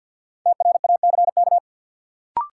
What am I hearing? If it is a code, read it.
Morse code, heard as a keyed tone, about 49 wpm, 700 Hz. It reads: TRACK
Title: Sentences from Top 1100 Words - New Word Drill Alt - 50wpm